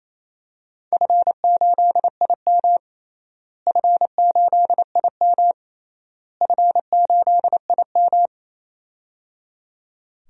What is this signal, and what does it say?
Morse code, heard as a keyed tone, about 28 wpm, 700 Hz. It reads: F8IM F8IM F8IM
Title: DXCC-100 Call Signs - 4 Characters - Repeated 3x - 28wpm